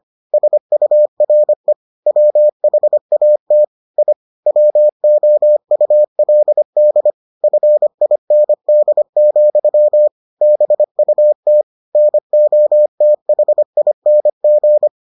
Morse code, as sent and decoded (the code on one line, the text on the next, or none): ... ..- .-. / .-- .... .- - / .. / .-- --- ..- .-.. -.. / ..-. .. -. -.. --..-- / -... ..- - / -. --- - .... .. -. --.
SURE WHAT I WOULD FIND, BUT NOTHING